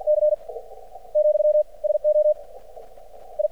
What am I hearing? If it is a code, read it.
Morse code, heard as a keyed tone, about 50 wpm, 600 Hz. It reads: C , SO I
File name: archived / CRSnet